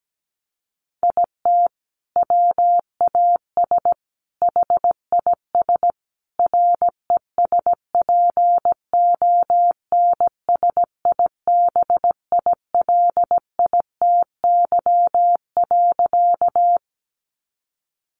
Morse code, heard as a keyed tone, about 17 wpm, 700 Hz. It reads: IT WAS HIS RESPONSIBILITY.
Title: Sentences from Top 1200 Words 17wpm